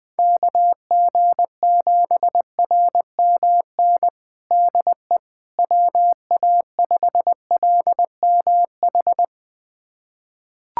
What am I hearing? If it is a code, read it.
Morse code, heard as a keyed tone, about 20 wpm, 700 Hz. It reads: KG7RMN DE WA5LMH
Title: USA Callsign DE Callsign - 6 Characters - 20wpm